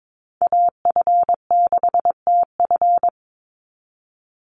Morse code, heard as a keyed tone, about 22 wpm, 700 Hz. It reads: AF6TF
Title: USA Call Signs 22wpm